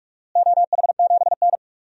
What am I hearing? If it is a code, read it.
Morse code, heard as a keyed tone, about 47 wpm, 700 Hz. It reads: OH7N
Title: DXCC-100 Call Signs - 4 Characters - Repeated 3x - 45wpm